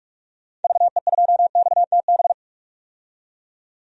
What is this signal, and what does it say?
Morse code, heard as a keyed tone, about 43 wpm, 700 Hz. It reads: VE2XTB
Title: DXCC-100 Call Signs - 6 Characters - 45wpm